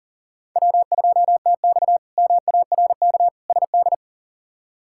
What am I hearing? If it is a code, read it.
Morse code, heard as a keyed tone, about 37 wpm, 700 Hz. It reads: W2TX MARK SD
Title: K1USN SST QSO Practice - 40wpm